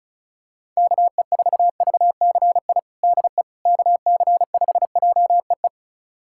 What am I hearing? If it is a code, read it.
Morse code, heard as a keyed tone, about 34 wpm, 700 Hz. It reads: KE4VCI DE KC5JEE
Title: USA Callsign DE Callsign - 6 Characters - 35wpm